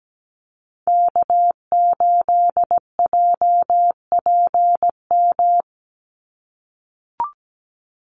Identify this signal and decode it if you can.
Morse code, heard as a keyed tone, about 17 wpm, 700 Hz. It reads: K8JPM
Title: Call Signs - 17wpm